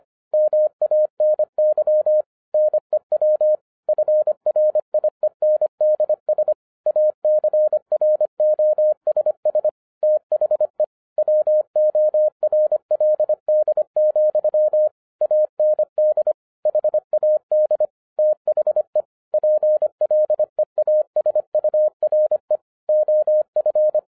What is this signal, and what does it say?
Morse code, heard as a keyed tone, about 24 wpm, 610 Hz. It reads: MANY NEW FRIENDS ACROSS THE WORLD, AND HAD THE PLEASURE OF